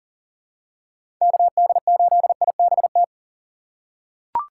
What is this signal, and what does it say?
Morse code, heard as a keyed tone, about 40 wpm, 700 Hz. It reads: KD8IBT
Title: USA Call Signs - 6 Characters - 2x Character Spacing - 40wpm